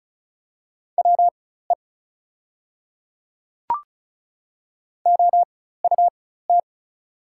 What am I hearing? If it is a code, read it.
Morse code, heard as a keyed tone, about 36 wpm, spaced as for 16 wpm, 700 Hz. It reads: WE OUT
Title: 4x Character Spacing - Top 100 Words - 35wpm